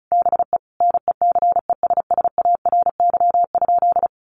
Morse code, heard as a keyed tone, about 34 wpm, 700 Hz. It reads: BE NECESSARY?
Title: Sentences from Top 800 Words 35wpm